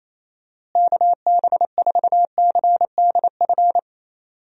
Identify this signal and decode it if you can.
Morse code, heard as a keyed tone, about 27 wpm, 700 Hz. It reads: KB4CDF
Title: USA Call Signs - 6 Characters - Repeated 2x - 28wpm